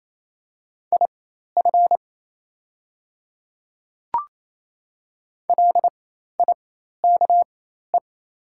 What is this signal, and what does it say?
Morse code, heard as a keyed tone, about 28 wpm, spaced as for 13 wpm, 700 Hz. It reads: IF LIKE
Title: 4x Character Spacing - Top 100 Words - 28wpm